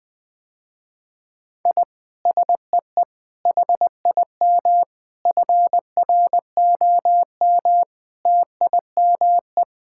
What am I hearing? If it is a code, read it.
Morse code, heard as a keyed tone, about 20 wpm, 700 Hz. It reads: I SEE HIM FROM TIME
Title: Sentences with 4-Word Phrases - Spoken Beginning - 20wpm